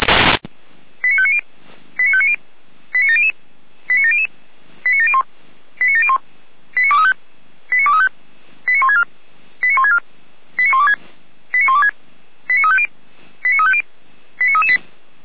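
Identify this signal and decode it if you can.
Five-tone selective call (ZVEI), heard as quick runs of five tones, tones of about 70 ms. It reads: 89590 89590 89700 89700 89701 89701 89235 89235 89164 89164 89126 89126 89350 89350 89338
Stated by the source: ZVEI five-tone selective call 89094, received by SDR.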